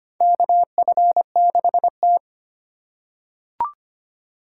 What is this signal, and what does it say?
Morse code, heard as a keyed tone, about 25 wpm, 700 Hz. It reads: KF6T
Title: K1USN SST QSO Practice - 25wpm